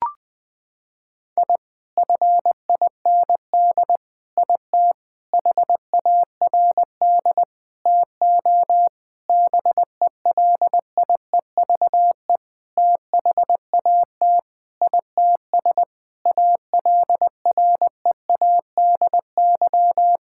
Morse code, heard as a keyed tone, about 21 wpm, 700 Hz. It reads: I FIND IT HARD TO BELIEVE THAT ITS ALREADY
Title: Sentences with 7-Word Phrases - 20wpm